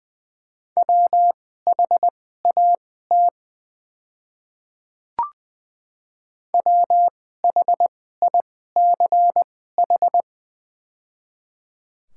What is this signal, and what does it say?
Morse code, heard as a keyed tone, about 20 wpm, spaced as for 14 wpm, 690 Hz. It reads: WHAT WHICH
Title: 2x Character Spacing - Top 100 Words - 20wpm